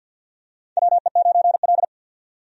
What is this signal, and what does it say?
Morse code, heard as a keyed tone, about 49 wpm, 700 Hz. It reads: WE9L
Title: USA Call Signs - 4 Characters - 50wpm